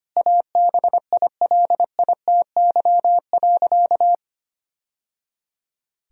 Morse code, heard as a keyed tone, about 25 wpm, 700 Hz. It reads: ABILITY.
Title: Sentences from Top 1000 Words - Review Part 3 - 25wpm